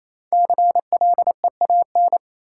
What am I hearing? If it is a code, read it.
Morse code, heard as a keyed tone, about 28 wpm, 700 Hz. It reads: CLEAN
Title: Sentences from Top 1100 Words - New Word Drill - 28wpm